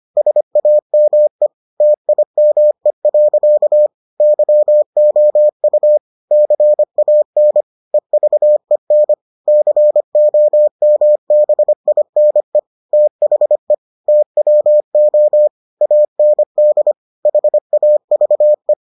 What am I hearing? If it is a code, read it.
Morse code, heard as a keyed tone, about 25 wpm, 600 Hz. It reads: SAME TIME. YOU CAN EVEN COMBINE THE TWO AND HAVE